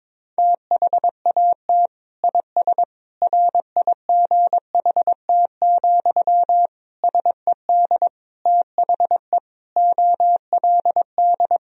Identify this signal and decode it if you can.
Morse code, heard as a keyed tone, about 22 wpm, 700 Hz. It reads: THAT IS RIGHT, SED THE OLD
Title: Sentences from Top 200 Words and Abbreviations 22wpm